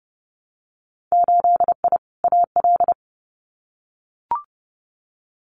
Morse code, heard as a keyed tone, about 30 wpm, 700 Hz. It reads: 8I AL
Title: Winter Field Day Exchanges Encore 30wpm